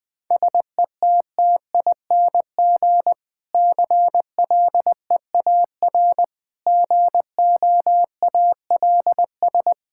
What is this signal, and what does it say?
Morse code, heard as a keyed tone, about 21 wpm, 700 Hz. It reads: SETTING CLEAR GOALS
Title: Difficult Sentences with Hints - 20wpm